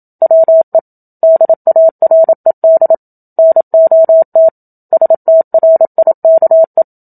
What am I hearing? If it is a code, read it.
Morse code, heard as a keyed tone, about 27 wpm, 650 Hz. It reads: WE DARED NOT STRIKE
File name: Book Two / 01-Chapter 1 - 27 WPM